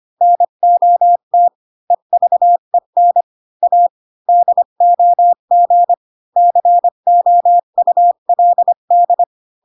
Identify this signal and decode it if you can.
Morse code, heard as a keyed tone, about 25 wpm, 700 Hz. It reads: NOT EVEN A DOG COULD